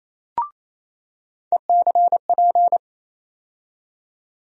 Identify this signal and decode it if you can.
Morse code, heard as a keyed tone, about 28 wpm, 700 Hz. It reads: ECP
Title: USA Call Sign Postfixes - 3 Letters 28wpm